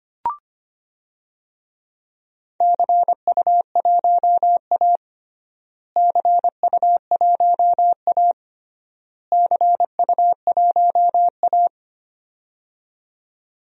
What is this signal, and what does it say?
Morse code, heard as a keyed tone, about 25 wpm, 700 Hz. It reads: CU1A CU1A CU1A
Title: DXCC-100 Call Signs - 4 Characters - Repeated 3x - 25wpm